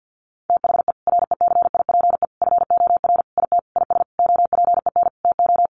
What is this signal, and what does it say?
Morse code, heard as a keyed tone, about 50 wpm, 700 Hz. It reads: THE RECIPE FOR IT IS OPEN TO
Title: Sentences from Top 1200 Words 50wpm